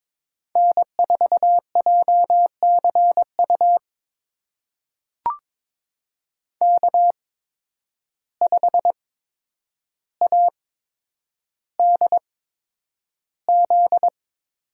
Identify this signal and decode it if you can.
Morse code, heard as a keyed tone, about 22 wpm, spaced as for 9 wpm, 700 Hz. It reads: N4JCU K5ADZ
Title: USA Call Signs - 5 Characters - 8x Character Spacing - 22wpm